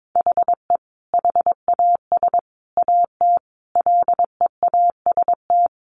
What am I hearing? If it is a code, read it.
Morse code, heard as a keyed tone, about 22 wpm, 700 Hz. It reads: HE HAS AT LEAST
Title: Sentences from Top 1500 Words 22wpm